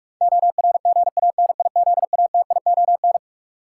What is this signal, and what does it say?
Morse code, heard as a keyed tone, about 41 wpm, 700 Hz. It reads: ORGANIZATION
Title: Sentences from Top 1000 Words - Review Part 5 - New Word Drill - 45wpm